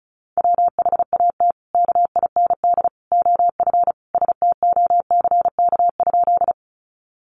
Morse code, heard as a keyed tone, about 34 wpm, 700 Hz. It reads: WHAT KIND OF STOCK?
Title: Sentences from Top 800 Words 35wpm